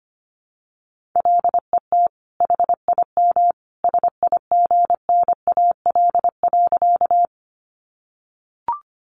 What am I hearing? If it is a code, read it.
Morse code, heard as a keyed tone, about 25 wpm, 700 Hz. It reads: LET HIM SIGNAL.
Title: Sentences from Top 1400 Words 25wpm